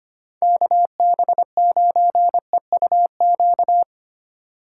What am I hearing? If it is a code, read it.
Morse code, heard as a keyed tone, about 25 wpm, 700 Hz. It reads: KB9EUQ
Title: USA Call Signs - 2x Character Spacing - 25wpm